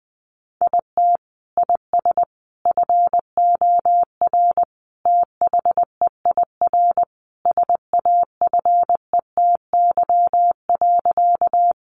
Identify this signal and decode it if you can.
Morse code, heard as a keyed tone, about 20 wpm, 700 Hz. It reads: IT IS FOR THEIR SAFETY.
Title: Sentences from Top 700 Words 20wpm